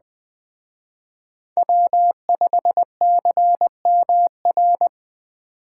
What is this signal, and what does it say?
Morse code, heard as a keyed tone, about 21 wpm, 700 Hz. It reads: W5CMR
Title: Speed Racing - USA 1x3 Call Signs 17wpm